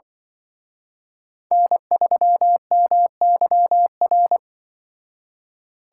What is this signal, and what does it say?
Morse code, heard as a keyed tone, about 24 wpm, 700 Hz. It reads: N3MYR
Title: Speed Racing - USA 1x3 Call Signs 20wpm